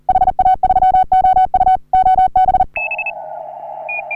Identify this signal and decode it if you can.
Morse code, heard as a keyed tone, about 39 wpm, 750 Hz. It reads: HA3OU OB
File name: CW / HA3OU RHOBCW REC